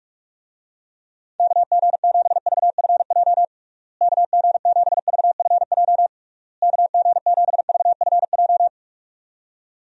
Morse code, heard as a keyed tone, about 46 wpm, 700 Hz. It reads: KG7VFJ KG7VFJ KG7VFJ
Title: USA Call Signs - 6 Characters - Repeated 3x - 45wpm